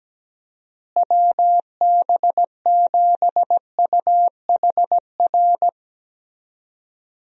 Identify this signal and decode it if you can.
Morse code, heard as a keyed tone, about 17 wpm, 700 Hz. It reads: WB7UHR
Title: USA Call Signs - 6 Characters - Repeated 3x - 17wpm